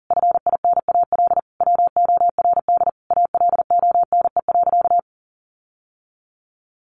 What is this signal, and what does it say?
Morse code, heard as a keyed tone, about 40 wpm, 700 Hz. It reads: FINAL WORD ALONE.
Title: Sentences from Top 1000 Words 40wpm